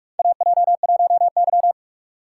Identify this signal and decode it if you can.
Morse code, heard as a keyed tone, about 46 wpm, 700 Hz. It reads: AJ1Y
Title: USA Call Signs - 4 Characters - 6x Character Spacing - 45wpm